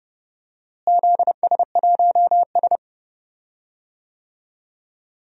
Morse code, heard as a keyed tone, about 31 wpm, 700 Hz. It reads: ZS1S
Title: DXCC-100 Call Signs - 4 Characters - Repeated 2x - 30wpm